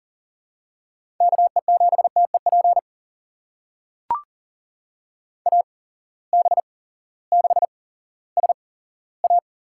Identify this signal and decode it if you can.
Morse code, heard as a keyed tone, about 40 wpm, spaced as for 18 wpm, 700 Hz. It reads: KE7TEP AB6SA
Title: USA Call Signs - 8x Character Spacing - 40wpm